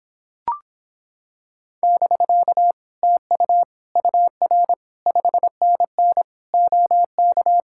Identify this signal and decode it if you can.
Morse code, heard as a keyed tone, about 26 wpm, 700 Hz. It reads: <BK> TU UR 5NN OK